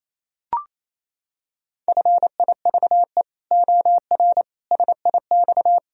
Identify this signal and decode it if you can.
Morse code, heard as a keyed tone, about 27 wpm, 700 Hz. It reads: FIVE OR SIX
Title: Sentences from Top 1000 Words - Review Part 3 - 28wpm